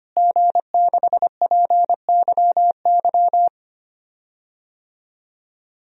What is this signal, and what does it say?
Morse code, heard as a keyed tone, about 25 wpm, 700 Hz. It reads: G6PYY
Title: DXCC-100 Call Signs - 5 Characters - Repeated 3x - 25wpm